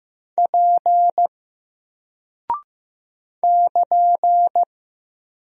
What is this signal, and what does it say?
Morse code, heard as a keyed tone, about 15 wpm, 700 Hz. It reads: P <KN>
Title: Single Character (Letter:Number:Punctuation:Prosign) Alt 15wpm